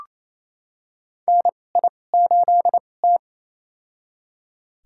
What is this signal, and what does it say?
Morse code, heard as a keyed tone, about 27 wpm, spaced as for 20 wpm, 690 Hz. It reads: NI8T
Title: USA Call Signs - 4 Characters - 2x Character Spacing - 28wpm